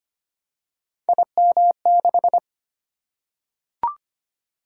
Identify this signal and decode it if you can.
Morse code, heard as a keyed tone, about 25 wpm, 700 Hz. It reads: IM6
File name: Three Letter-Number Combinations - 25wpm